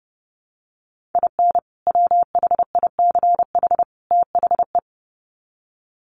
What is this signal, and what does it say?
Morse code, heard as a keyed tone, about 30 wpm, 700 Hz. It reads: IN WHICH THE
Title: Sets of 3 Words 30wpm